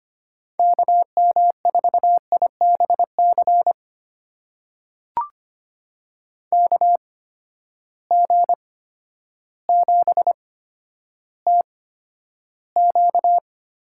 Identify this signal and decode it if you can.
Morse code, heard as a keyed tone, about 25 wpm, spaced as for 11 wpm, 700 Hz. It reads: KM4IBC KG7TQ
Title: USA Call Signs - 6 Characters - 8x Character Spacing - 25wpm